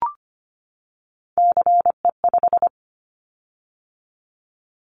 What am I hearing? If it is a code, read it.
Morse code, heard as a keyed tone, about 25 wpm, 700 Hz. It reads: CE5
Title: Callsign Prefixes for Top 100 DXCC - 25wpm